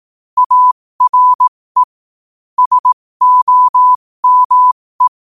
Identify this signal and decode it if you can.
Morse code, heard as a keyed tone, about 18 wpm, spaced as for 15 wpm, 1000 Hz. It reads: ARE SOME
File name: qotdfiles / 15wpm